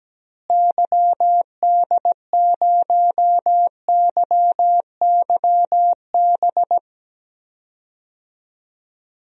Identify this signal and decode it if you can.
Morse code, heard as a keyed tone, about 17 wpm, 700 Hz. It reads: YD0YYB
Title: DXCC-100 Call Signs - 6 Characters - Repeated 3x - 17wpm